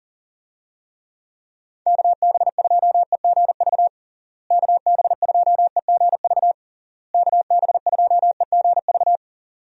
Morse code, heard as a keyed tone, about 40 wpm, 700 Hz. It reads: KB2EGV KB2EGV KB2EGV
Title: USA Call Signs - 6 Characters - Repeated 3x - 40wpm